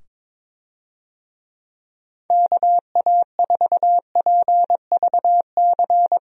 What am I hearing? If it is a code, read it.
Morse code, heard as a keyed tone, about 22 wpm, 700 Hz. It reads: KA4PVC
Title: USA Call Signs - 6 Characters - 22wpm